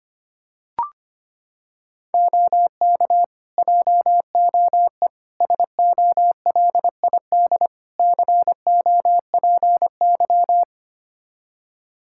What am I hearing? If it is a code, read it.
Morse code, heard as a keyed tone, about 25 wpm, 700 Hz. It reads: OK JOE SOLID COPY
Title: K1USN SST QSO Practice 25wpm